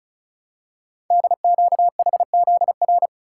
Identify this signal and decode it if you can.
Morse code, heard as a keyed tone, about 36 wpm, 700 Hz. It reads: DQHZR
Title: Code Groups - 5 Letter:Numbers - 35wpm